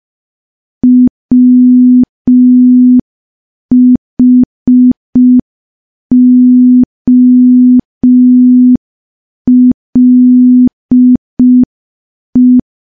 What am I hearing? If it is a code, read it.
Morse code, heard as a keyed tone, about 5 wpm, 250 Hz. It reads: WHOLE
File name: morse / morse-training-05WPM